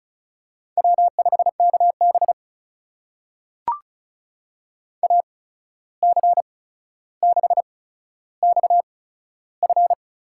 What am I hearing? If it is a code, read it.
Morse code, heard as a keyed tone, about 36 wpm, spaced as for 14 wpm, 700 Hz. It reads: W5KB AC6XF
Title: USA Call Signs - 8x Character Spacing - 35wpm